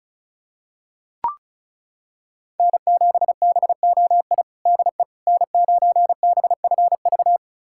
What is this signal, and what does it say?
Morse code, heard as a keyed tone, about 36 wpm, 700 Hz. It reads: N7BOI DE N9BFV
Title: USA Callsign DE Callsign - 5 Characters - 35wpm